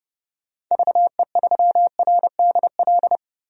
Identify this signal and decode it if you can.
Morse code, heard as a keyed tone, about 32 wpm, 700 Hz. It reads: VE3RDL
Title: DXCC-100 Call Signs - 6 Characters - 30wpm